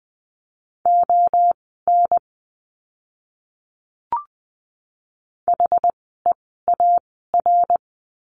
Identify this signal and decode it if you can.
Morse code, heard as a keyed tone, about 20 wpm, spaced as for 15 wpm, 700 Hz. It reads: ON HEAR
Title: 244 Lesson 39f - Letter z - Words 20wpm